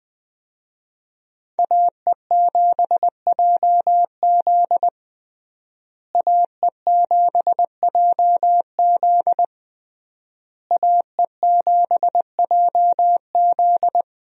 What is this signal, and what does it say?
Morse code, heard as a keyed tone, about 21 wpm, 700 Hz. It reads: AE7JZ AE7JZ AE7JZ
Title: USA Call Signs - 5 Characters - Repeated 3x - 20wpm